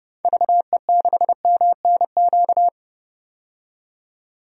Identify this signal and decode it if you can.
Morse code, heard as a keyed tone, about 31 wpm, 700 Hz. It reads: VE6MNQ
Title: DXCC-100 Call Signs - 6 Characters - 30wpm